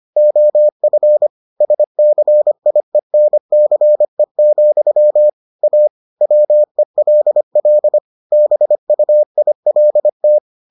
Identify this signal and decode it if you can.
Morse code, heard as a keyed tone, about 25 wpm, 600 Hz. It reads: OF SCIENCE, A WELL BUILT